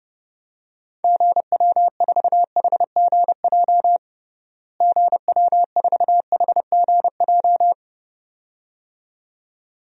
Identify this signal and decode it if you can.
Morse code, heard as a keyed tone, about 30 wpm, 700 Hz. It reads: GW4HGJ GW4HGJ
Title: DXCC-100 Call Signs - 6 Characters - Repeated 2x - 30wpm